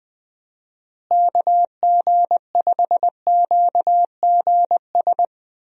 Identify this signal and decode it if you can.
Morse code, heard as a keyed tone, about 20 wpm, 700 Hz. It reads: KG5QGS
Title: USA Call Signs - 6 Characters - Repeated 3x - 20wpm